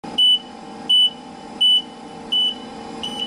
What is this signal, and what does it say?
There is no code.